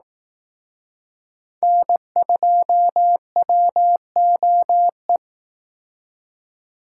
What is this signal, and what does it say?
Morse code, heard as a keyed tone, about 18 wpm, 700 Hz. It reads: N2WOE
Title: Speed Racing - USA Call Signs 15wpm